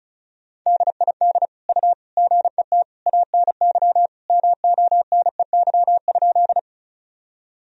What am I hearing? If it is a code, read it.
Morse code, heard as a keyed tone, about 36 wpm, 700 Hz. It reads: DID U GET ANY MONEY?
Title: Sentences from Top 400 Words and Abbreviations 35wpm